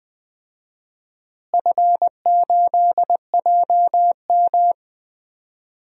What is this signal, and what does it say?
Morse code, heard as a keyed tone, about 20 wpm, 700 Hz. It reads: F8JM
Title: DXCC-100 Call Signs - 4 Characters - Repeated 2x - 20wpm